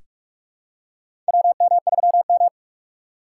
Morse code, heard as a keyed tone, about 44 wpm, 690 Hz. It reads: WM3M
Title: USA Call Signs - 4 Characters - 3x Character Spacing - 45wpm